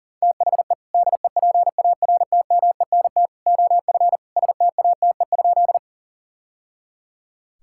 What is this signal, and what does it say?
Morse code, heard as a keyed tone, about 40 wpm, 700 Hz. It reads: THE DEPARTMENT OF STATE?
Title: Sentences from Top 1000 Words - Review Part 5 - 40wpm